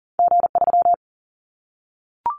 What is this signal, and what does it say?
Morse code, heard as a keyed tone, about 38 wpm, 700 Hz. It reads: G3
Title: Callsign Prefixes for Top 25 DXCC - Repeated 3x - 40wpm